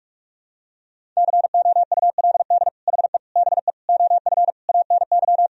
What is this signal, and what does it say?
Morse code, heard as a keyed tone, about 46 wpm, 700 Hz. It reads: COULD HE BE OF ANY
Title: Sentences from Top 1000 Words - Review Part 4 - 45wpm